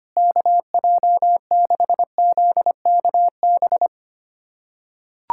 Morse code, heard as a keyed tone, about 25 wpm, 700 Hz. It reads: KJ6ZKB
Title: USA Callsign DE Callsign 25wpm